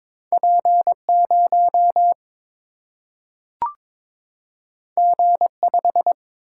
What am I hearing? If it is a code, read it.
Morse code, heard as a keyed tone, about 22 wpm, 700 Hz. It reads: P 0 G 5
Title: Single-Letter Single-Number 22wpm